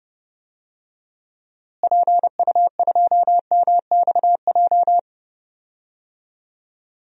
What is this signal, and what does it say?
Morse code, heard as a keyed tone, about 31 wpm, 700 Hz. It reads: PU2MXJ
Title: DXCC-100 Call Signs - 6 Characters - Repeated 2x - 30wpm